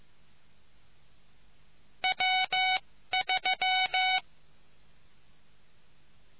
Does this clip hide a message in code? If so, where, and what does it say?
Yes: Morse code from 2.0 to 4.2 s, heard as a keyed tone, about 15 wpm, spaced as for 13 wpm, 750 Hz. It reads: W3